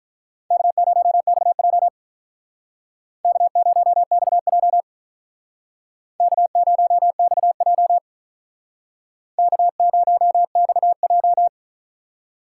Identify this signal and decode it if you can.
Morse code, heard as a keyed tone, about 46 wpm, 700 Hz. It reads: K0XJ K0XJ K0XJ K0XJ